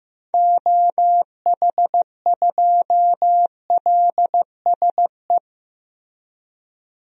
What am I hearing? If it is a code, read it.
Morse code, heard as a keyed tone, about 15 wpm, 700 Hz. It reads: OH2LSE
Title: DXCC-100 Call Signs - 6 Characters - Repeated 3x - 15wpm